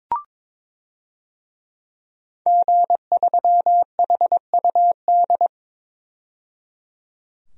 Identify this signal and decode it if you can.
Morse code, heard as a keyed tone, about 22 wpm, 700 Hz. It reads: G3HUD
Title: DXCC-100 Call Signs - 5 Characters - 22wpm